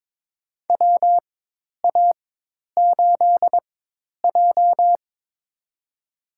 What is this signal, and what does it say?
Morse code, heard as a keyed tone, about 22 wpm, spaced as for 10 wpm, 700 Hz. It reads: WA8J
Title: USA Call Signs - 4x Character Spacing - 22wpm